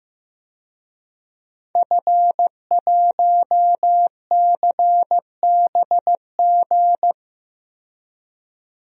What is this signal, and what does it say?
Morse code, heard as a keyed tone, about 16 wpm, 700 Hz. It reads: F1CBG
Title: DXCC-100 Call Signs - 5 Characters - 15wpm